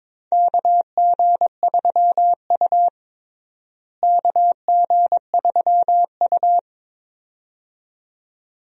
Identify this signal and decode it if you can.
Morse code, heard as a keyed tone, about 22 wpm, 700 Hz. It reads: KG3U KG3U
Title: USA Call Signs - 4 Characters - Repeated 2x - 22wpm